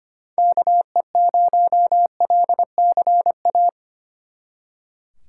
Morse code, heard as a keyed tone, about 33 wpm, 700 Hz. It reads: KE0LCA